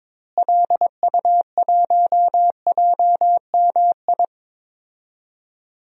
Morse code, heard as a keyed tone, about 21 wpm, 700 Hz. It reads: LU1JMI
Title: DXCC-100 Call Signs - 6 Characters - Repeated 2x - 22wpm